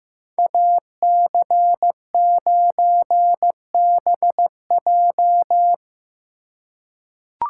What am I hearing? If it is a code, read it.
Morse code, heard as a keyed tone, about 15 wpm, 700 Hz. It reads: AC9BJ
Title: USA Call Signs - 5 Characters - 3x Character Spacing - 15wpm